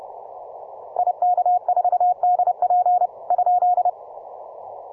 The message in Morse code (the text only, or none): IK4DP ?